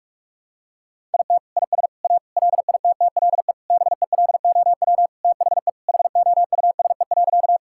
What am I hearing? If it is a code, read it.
Morse code, heard as a keyed tone, about 46 wpm, 700 Hz. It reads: IT IS A LITTLE BELOW THE HOUSE.
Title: Sentences from Top 1000 Words - Review Part 5 - 45wpm